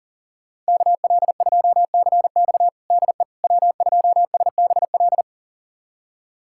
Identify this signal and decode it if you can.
Morse code, heard as a keyed tone, about 40 wpm, 700 Hz. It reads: KL2CX DE W2SBL
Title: USA Callsign DE Callsign - 5 Characters - 40wpm